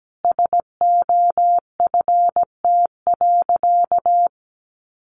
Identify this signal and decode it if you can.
Morse code, heard as a keyed tone, about 17 wpm, 700 Hz. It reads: SOFT.
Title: Sentences from Top 1500 Words 17wpm